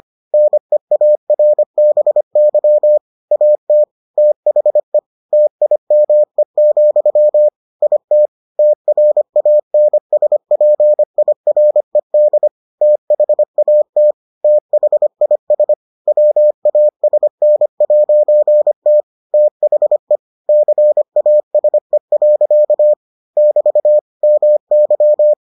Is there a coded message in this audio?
Yes: Morse code from 0.3 to 25.3 s, heard as a keyed tone, about 25 wpm, 600 Hz. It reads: NEARBY AT THE TIME, IT TRANSPIRED THAT THIS WASN'T THE CASE. <BT> MY